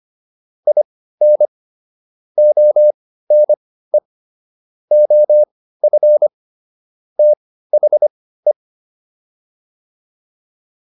Morse code, heard as a keyed tone, about 25 wpm, spaced as for 15 wpm, 600 Hz. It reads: IN ONE OF THE